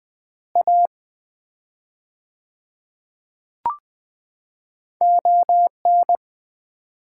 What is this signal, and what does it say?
Morse code, heard as a keyed tone, about 20 wpm, 700 Hz. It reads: A ON
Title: 4x Word-Spacing - Sets of 4 Words - 20wpm